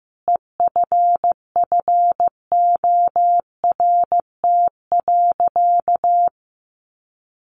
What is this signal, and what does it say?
Morse code, heard as a keyed tone, about 15 wpm, 700 Hz. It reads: EFFORT.
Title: Sentences from Top 1200 Words 15wpm